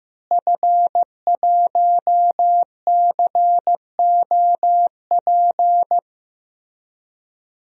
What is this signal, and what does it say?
Morse code, heard as a keyed tone, about 15 wpm, 700 Hz. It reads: F1COP
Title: DXCC-100 Call Signs - 5 Characters - Repeated 3x - 15wpm